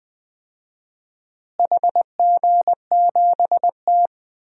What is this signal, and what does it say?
Morse code, heard as a keyed tone, about 20 wpm, 700 Hz. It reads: HG7T
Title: DXCC-100 Call Signs - 4 Characters - Repeated 2x - 20wpm